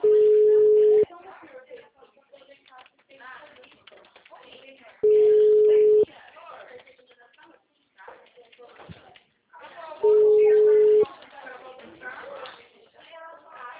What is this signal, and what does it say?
Ringback tone (Europe, 425 Hz), with speech.
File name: out- -2030- - - .23790